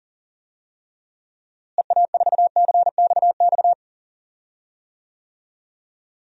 Morse code, heard as a keyed tone, about 38 wpm, 700 Hz. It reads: EA4CXX
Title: DXCC-100 Call Signs - 6 Characters - Repeated 3x - 40wpm